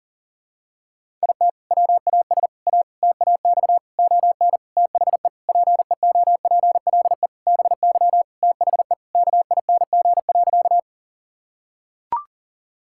Morse code, heard as a keyed tone, about 40 wpm, 700 Hz. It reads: IT WAS A TAX ON THE PEOPLE BY THE KING.
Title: Sentences from Top 1000 Words - Review Part 5 - 40wpm